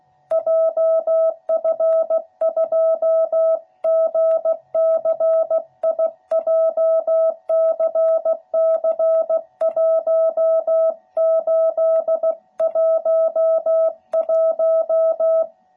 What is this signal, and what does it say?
Morse code, heard as a keyed tone, about 16 wpm, 660 Hz. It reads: JF2GCI JCC1811
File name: jh8cbh / 2021cw / cw12b